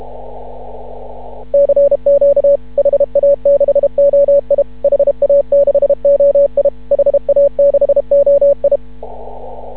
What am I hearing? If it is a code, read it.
Morse code, heard as a keyed tone, about 33 wpm, 560 Hz. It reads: CQ HA6OI HA6OI HA6OI
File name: CW / ha6oi 2